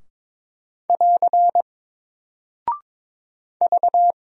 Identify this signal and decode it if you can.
Morse code, heard as a keyed tone, about 21 wpm, 700 Hz. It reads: <AR> V
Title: Single Character (Letter:Number:Punctuation:Prosign) Alt 22wpm